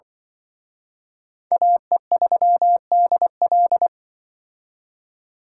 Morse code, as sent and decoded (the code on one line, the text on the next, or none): .- . ...-- -.. .-..
AE3DL